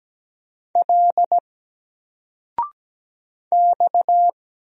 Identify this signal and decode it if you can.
Morse code, heard as a keyed tone, about 17 wpm, 700 Hz. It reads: L X
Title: Single Letter-Number 17wpm